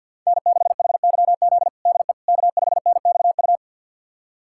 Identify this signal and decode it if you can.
Morse code, heard as a keyed tone, about 46 wpm, 700 Hz. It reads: N6HYZ DE K5NXU
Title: USA Callsign DE Callsign - 5 Characters - 50wpm